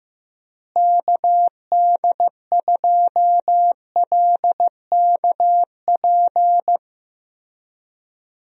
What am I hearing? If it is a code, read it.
Morse code, heard as a keyed tone, about 15 wpm, 700 Hz. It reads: KD2LKP
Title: USA Call Signs - 6 Characters - Repeated 3x - 15wpm